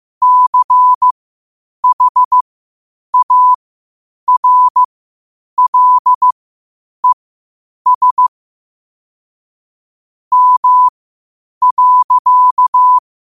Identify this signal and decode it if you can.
Morse code, heard as a keyed tone, about 15 wpm, spaced as for 8 wpm, 1000 Hz. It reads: CHARLES M.
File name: qotdfiles / 8wpm